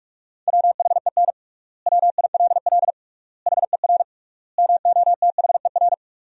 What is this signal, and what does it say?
Morse code, heard as a keyed tone, about 46 wpm, 700 Hz. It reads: WHEN WILL HER MOTHER
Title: 3x Word-Spacing - Sentences from Top 200 Words - 45wpm